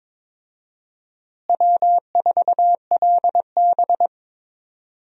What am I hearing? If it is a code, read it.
Morse code, heard as a keyed tone, about 22 wpm, 700 Hz. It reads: W4LB
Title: USA Call Signs - 4 Characters - Repeated 2x - 22wpm